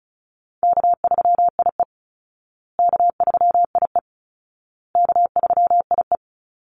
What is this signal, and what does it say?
Morse code, heard as a keyed tone, about 34 wpm, 700 Hz. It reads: K3IE K3IE K3IE
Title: Easy Call Signs Repeated 3x - 35wpm